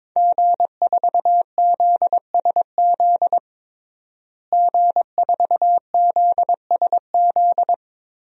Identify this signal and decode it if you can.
Morse code, heard as a keyed tone, about 22 wpm, 700 Hz. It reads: G4ZSZ G4ZSZ
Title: DXCC-100 Call Signs - 5 Characters - Repeated 3x - 22wpm